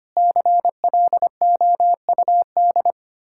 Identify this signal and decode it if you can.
Morse code, heard as a keyed tone, about 33 wpm, 700 Hz. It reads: CLOUD